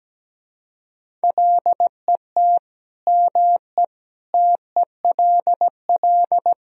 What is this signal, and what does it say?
Morse code, heard as a keyed tone, about 17 wpm, 700 Hz. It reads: LET ME TELL
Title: Speed-Racing - Sets of 4 Words 17wpm